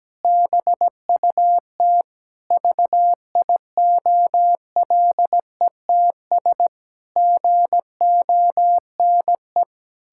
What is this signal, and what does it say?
Morse code, heard as a keyed tone, about 17 wpm, 700 Hz. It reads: BUT VIOLETS GONE
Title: Pangram Challenge v2 - 17wpm